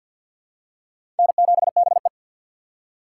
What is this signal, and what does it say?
Morse code, heard as a keyed tone, about 52 wpm, 700 Hz. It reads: N7BE
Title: USA Call Signs - 3x Character Spacing - 50wpm